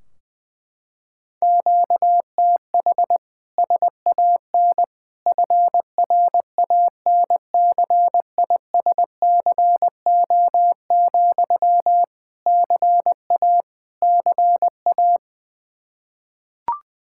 Morse code, heard as a keyed tone, about 21 wpm, 700 Hz. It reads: QTH SAN FRANCISCO, CA CA
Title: K1USN SST QSO Practice 20wpm